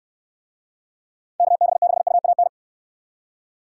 Morse code, heard as a keyed tone, about 64 wpm, 700 Hz. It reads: KB6UAN